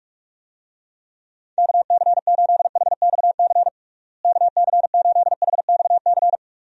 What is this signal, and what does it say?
Morse code, heard as a keyed tone, about 41 wpm, 700 Hz. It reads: KC8HXC KC8HXC
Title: USA Call Signs - 6 Characters - Repeated 2x - 45wpm